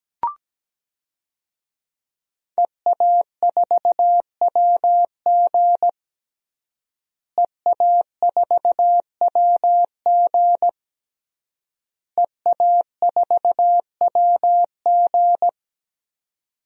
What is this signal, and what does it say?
Morse code, heard as a keyed tone, about 17 wpm, 700 Hz. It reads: EA4WG EA4WG EA4WG
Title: DXCC-100 Call Signs - 5 Characters - Repeated 3x - 17wpm